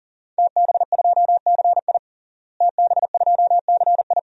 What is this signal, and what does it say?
Morse code, heard as a keyed tone, about 37 wpm, 700 Hz. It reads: TB2CI TB2CI
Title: DXCC-100 Call Signs - 5 Characters - Repeated 2x - 40wpm